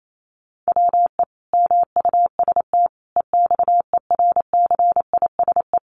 Morse code, heard as a keyed tone, about 28 wpm, 700 Hz. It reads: WE MUST EXERCISE
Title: Sentences from Top 1200 Words 28wpm